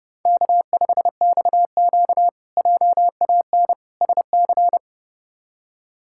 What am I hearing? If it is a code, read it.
Morse code, heard as a keyed tone, about 30 wpm, 700 Hz. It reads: K5XQ JAN SC
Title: K1USN SST QSO Practice - 30wpm